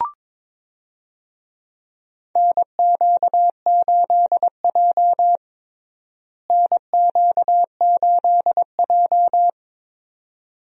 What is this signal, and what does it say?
Morse code, heard as a keyed tone, about 22 wpm, 700 Hz. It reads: NQ8J NQ8J
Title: USA Call Signs - 4 Characters - Repeated 2x - 22wpm